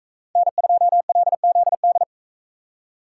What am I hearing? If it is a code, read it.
Morse code, heard as a keyed tone, about 43 wpm, 700 Hz. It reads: N2LZD